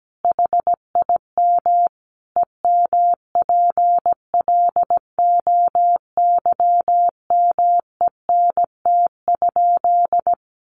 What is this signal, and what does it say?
Morse code, heard as a keyed tone, about 17 wpm, 700 Hz. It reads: HIM EMPLOYMENT?
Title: Sentences from Top 1500 Words 17wpm